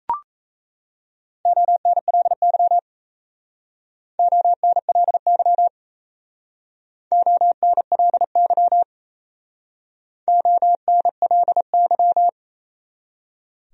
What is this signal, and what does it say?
Morse code, heard as a keyed tone, about 38 wpm, 700 Hz. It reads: ONLY ONLY ONLY ONLY